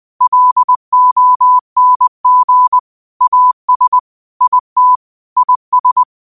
Morse code, heard as a keyed tone, about 20 wpm, 1000 Hz. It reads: LONG AS IT IS
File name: qotdfiles / 20wpm